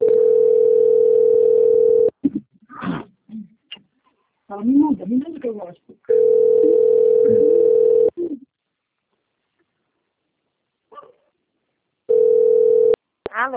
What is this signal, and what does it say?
Ringback tone (North America), 440 + 480 Hz, with speech.
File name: out- - - - - .26047